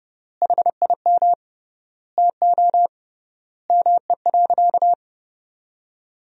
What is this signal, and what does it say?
Morse code, heard as a keyed tone, about 32 wpm, 700 Hz. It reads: HIM TO ME.
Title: 3x Word-Spacing - Sentences from Top 200 Words - 30wpm